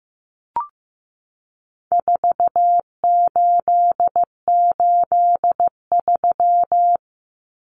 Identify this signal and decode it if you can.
Morse code, heard as a keyed tone, about 15 wpm, 700 Hz. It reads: 4883
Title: Numbers - 4-Digit 15wpm